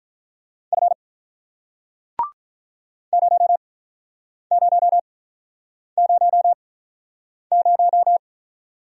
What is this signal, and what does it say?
Morse code, heard as a keyed tone, about 45 wpm, spaced as for 9 wpm, 700 Hz. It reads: F 0000